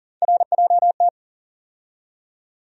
Morse code, heard as a keyed tone, about 40 wpm, 700 Hz. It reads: RJT
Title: USA Call Sign Postfixes - 3 Letters 40wpm